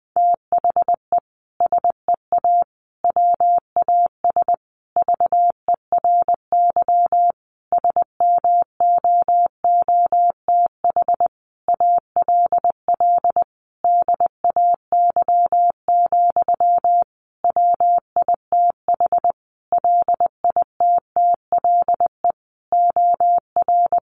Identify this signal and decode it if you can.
Morse code, heard as a keyed tone, about 20 wpm, 700 Hz. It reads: THE SEA WAS VERY SMOOTH ALL DAY, WITH LITTLE OR
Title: Sentences from Top 1400 Words 20wpm